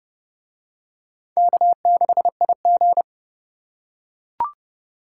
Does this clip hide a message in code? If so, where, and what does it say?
Yes: Morse code from 1.4 to 3.0 s, heard as a keyed tone, about 29 wpm, 700 Hz. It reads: K6IG